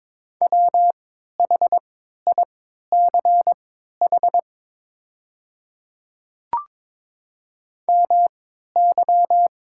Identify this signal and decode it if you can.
Morse code, heard as a keyed tone, about 22 wpm, spaced as for 12 wpm, 700 Hz. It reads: WHICH MY
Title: 3x Character Spacing - Top 300 Words - 22wpm